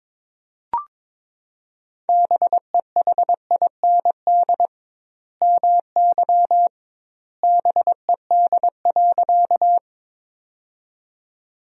Morse code, heard as a keyed tone, about 21 wpm, 700 Hz. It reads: BEHIND MY BED.
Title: 2x Word-Spacing - Sentences from Top 500 Words - 22wpm